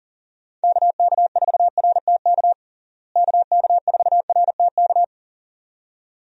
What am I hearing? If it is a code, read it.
Morse code, heard as a keyed tone, about 40 wpm, 700 Hz. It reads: KK4RTK KK4RTK
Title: USA Call Signs - 6 Characters - Repeated 2x - 40wpm